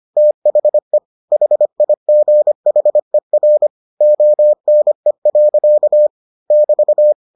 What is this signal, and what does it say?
Morse code, heard as a keyed tone, about 25 wpm, 600 Hz. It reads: THE HIGHER ONE. <BT>